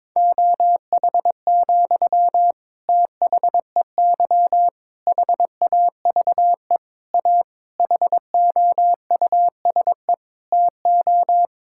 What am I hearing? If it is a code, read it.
Morse code, heard as a keyed tone, about 22 wpm, 700 Hz. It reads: OH, THEY HAVE A HOUSE TO
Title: Sentences from Top 1000 Words - Review Part 5 - 22wpm